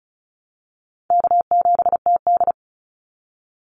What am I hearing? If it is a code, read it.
Morse code, heard as a keyed tone, about 34 wpm, 700 Hz. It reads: K7TD
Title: Easy Call Signs Repeated 3x - 35wpm